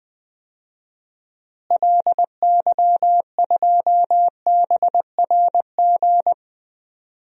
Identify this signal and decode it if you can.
Morse code, heard as a keyed tone, about 20 wpm, 700 Hz. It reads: LY2BRG
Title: DXCC-100 Call Signs - 6 Characters - Repeated 3x - 20wpm